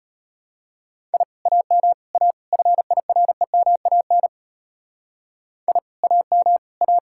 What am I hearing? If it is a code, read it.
Morse code, heard as a keyed tone, about 32 wpm, 700 Hz. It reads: I AM A FIREMAN I AM A